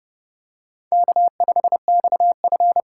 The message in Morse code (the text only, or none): K5XF